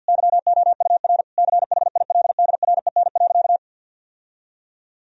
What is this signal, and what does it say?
Morse code, heard as a keyed tone, about 50 wpm, 700 Hz. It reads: YOUR CHILDREN.
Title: Sentences from Top 1000 Words - Review Part 3 - 50wpm